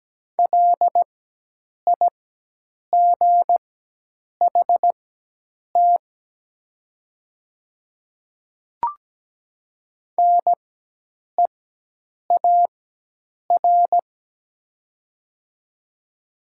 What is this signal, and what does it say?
Morse code, heard as a keyed tone, about 17 wpm, spaced as for 8 wpm, 700 Hz. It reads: LIGHT NEAR
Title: 4x Character Spacing - Top 300 Words - 17wpm